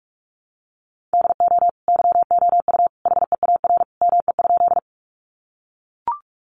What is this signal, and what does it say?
Morse code, heard as a keyed tone, about 44 wpm, 700 Hz. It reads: DO YOU HEAR ME?
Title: Sentences from Top 300 Words 45wpm